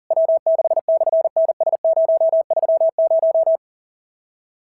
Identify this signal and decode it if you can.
Morse code, heard as a keyed tone, about 38 wpm, 660 Hz. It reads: W6/NS030
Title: SOTA QSO Practice 40wpm